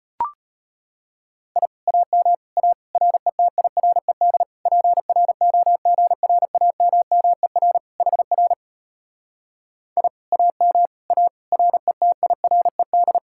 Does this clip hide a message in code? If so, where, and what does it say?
Yes: Morse code from 1.6 to 13.2 s, heard as a keyed tone, about 33 wpm, 700 Hz. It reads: I AM A RETIRED PROGRAMMER HR I AM A RETIRED